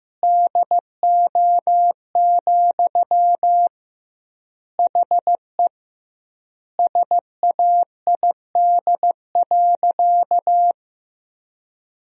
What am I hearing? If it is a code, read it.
Morse code, heard as a keyed tone, about 15 wpm, 700 Hz. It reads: DO, HE SAID.
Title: 2x Word-Spacing - Sentences from Top 100 Words - 15wpm